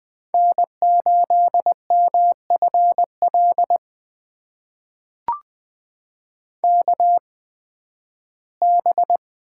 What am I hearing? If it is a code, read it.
Morse code, heard as a keyed tone, about 20 wpm, spaced as for 13 wpm, 700 Hz. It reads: N8MFL KB
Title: USA Call Signs - 8x Character Spacing - 20wpm